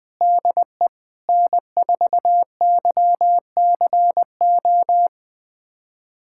Morse code, heard as a keyed tone, about 20 wpm, 700 Hz. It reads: DE N4YCO
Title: USA Callsign DE Callsign - 5 Characters - 20wpm